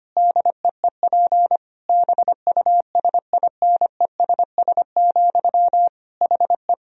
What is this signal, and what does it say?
Morse code, heard as a keyed tone, about 25 wpm, 700 Hz. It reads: DEEP BUSINESS, HE
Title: Sentences from Top 1000 Words - Review Part 3 - 25wpm